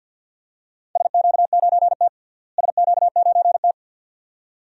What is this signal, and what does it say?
Morse code, heard as a keyed tone, about 49 wpm, 700 Hz. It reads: SQ9T SQ9T
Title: DXCC-100 Call Signs - 4 Characters - Repeated 2x - 50wpm